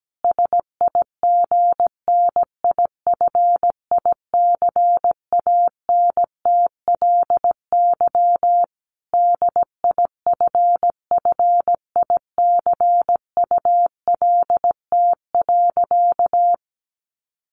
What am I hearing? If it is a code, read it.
Morse code, heard as a keyed tone, about 17 wpm, 700 Hz. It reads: SIGNIFICANTLY DIFFICULT.
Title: Sentences from Top 1500 Words 17wpm